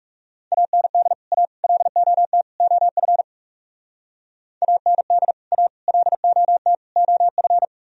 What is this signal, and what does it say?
Morse code, heard as a keyed tone, about 40 wpm, 700 Hz. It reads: AND A LOT OF AND A LOT OF